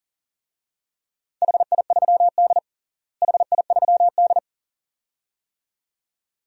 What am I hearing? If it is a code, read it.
Morse code, heard as a keyed tone, about 37 wpm, 700 Hz. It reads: HI3D HI3D
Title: DXCC-100 Call Signs - 4 Characters - Repeated 2x - 40wpm